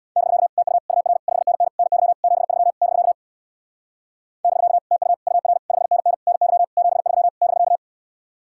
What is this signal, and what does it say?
Morse code, heard as a keyed tone, about 64 wpm, 700 Hz. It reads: <BK> TU UR 5NN NY 73 <BK> <BK> TU UR 5NN NY 73 <BK>